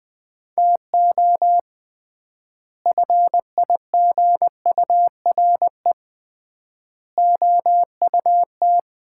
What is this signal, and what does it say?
Morse code, heard as a keyed tone, about 20 wpm, 700 Hz. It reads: TO FIGURE OUT
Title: 3x Word-Spacing - Sets of 5 Words - 20wpm